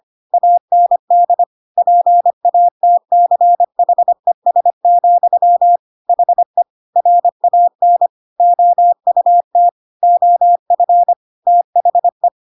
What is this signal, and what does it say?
Morse code, heard as a keyed tone, about 25 wpm, 700 Hz. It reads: AND PATCHES, HE RAN OUT OF THE